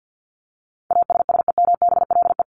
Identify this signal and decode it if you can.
Morse code, heard as a keyed tone, about 50 wpm, 700 Hz. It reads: ASSEMBLE
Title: LY Suffix - Top 100 Words with Base Words 50wpm